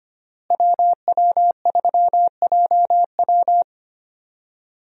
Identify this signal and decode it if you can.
Morse code, heard as a keyed tone, about 25 wpm, 700 Hz. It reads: WW3JW
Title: USA Call Signs - 5 Characters - Repeated 3x - 25wpm